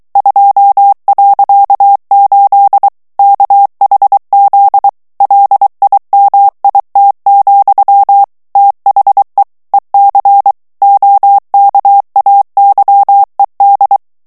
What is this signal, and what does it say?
Morse code, heard as a keyed tone, about 23 wpm, 800 Hz. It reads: 2.8 KHZ LIMIT, THE EC OKAYED
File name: ANGLAIS texte clair 3 vitesse 22 mots minute